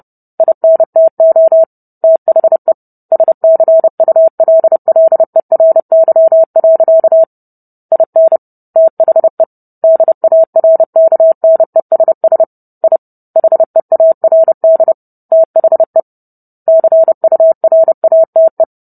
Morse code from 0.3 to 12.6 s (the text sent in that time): INTO THE SCULLERY. IN THE DARKNESS